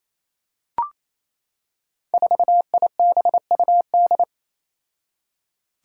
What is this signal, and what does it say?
Morse code, heard as a keyed tone, about 28 wpm, 700 Hz. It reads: 4IBUD
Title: Five Challenging Letter-Number Combinations 28wpm